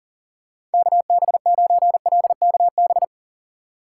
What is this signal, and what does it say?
Morse code, heard as a keyed tone, about 40 wpm, 700 Hz. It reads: KB9LKB
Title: USA Call Signs - 6 Characters - 4x Character Spacing - 40wpm